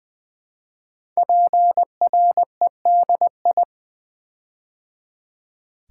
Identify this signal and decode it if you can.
Morse code, heard as a keyed tone, about 20 wpm, 700 Hz. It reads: PREDI
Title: Build A Word - 5 Letters At A Time - 10-Letter Words - 20wpm